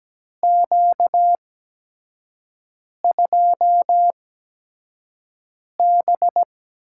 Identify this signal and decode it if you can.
Morse code, heard as a keyed tone, about 17 wpm, spaced as for 5 wpm, 700 Hz. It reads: Q2B
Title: USA Call Signs - 4 Characters - 8x Character Spacing - 17wpm